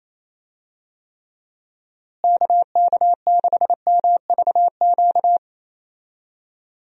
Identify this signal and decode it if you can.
Morse code, heard as a keyed tone, about 27 wpm, 700 Hz. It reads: KK6MVQ
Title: USA Call Signs - 6 Characters - Repeated 3x - 28wpm